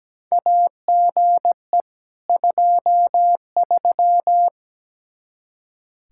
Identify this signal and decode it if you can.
Morse code, heard as a keyed tone, about 17 wpm, 700 Hz. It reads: AGE 23
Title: K1USN SST QSO Practice 17wpm